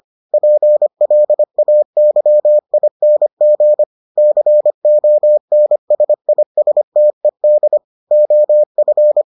Morse code, heard as a keyed tone, about 25 wpm, 600 Hz. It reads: PLAYING CONSISTED OF